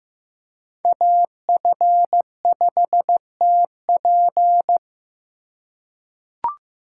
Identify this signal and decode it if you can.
Morse code, heard as a keyed tone, about 16 wpm, 700 Hz. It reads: AF5TP
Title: USA Call Signs - 3x Character Spacing - 15wpm